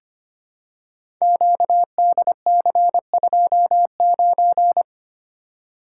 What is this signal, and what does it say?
Morse code, heard as a keyed tone, about 25 wpm, 700 Hz. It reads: QDC29
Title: Code Groups - 5 Letter:Numbers - 25wpm